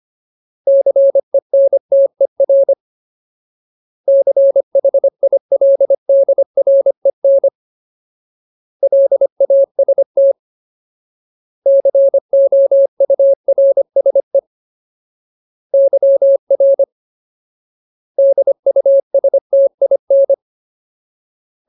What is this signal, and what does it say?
Morse code, heard as a keyed tone, about 25 wpm, 550 Hz. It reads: CENTER CHILDREN LAST COURSE YR DUSTIN
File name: Ultimate Word List / 25WPM Ultimate